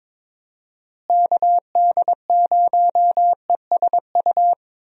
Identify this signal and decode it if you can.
Morse code, heard as a keyed tone, about 22 wpm, 700 Hz. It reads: KD0ESU
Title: USA Call Signs - 6 Characters - Repeated 3x - 22wpm